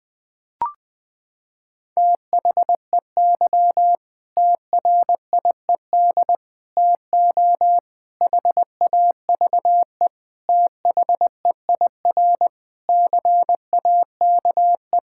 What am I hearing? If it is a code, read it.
Morse code, heard as a keyed tone, about 20 wpm, 700 Hz. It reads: THEY TRIED TO HAVE THEIR CAKE
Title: Sentences with 8-Word Phrases - 20wpm